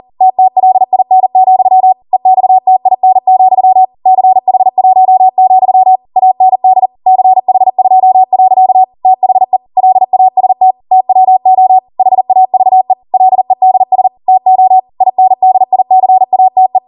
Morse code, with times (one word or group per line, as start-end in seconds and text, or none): 0.2-1.9 TTLIN,
2.1-3.9 EXTIN,
4.1-6.0 CH1,
6.2-6.9 AND
7.1-8.8 CH2.
9.1-9.6 THE
9.8-10.7 LAST
10.9-11.8 TWO
12.0-12.9 HAVE
13.1-14.1 LEDS
14.3-14.8 TO
15.0-16.8 INDICATE